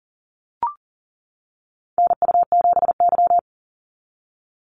Morse code, heard as a keyed tone, about 39 wpm, 700 Hz. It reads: NU7Y
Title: NA SOTA Activator Callsigns 40wpm